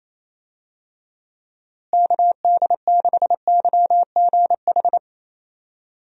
Morse code, heard as a keyed tone, about 28 wpm, 700 Hz. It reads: KD6YGH
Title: USA Call Signs - 6 Characters - Repeated 2x - 28wpm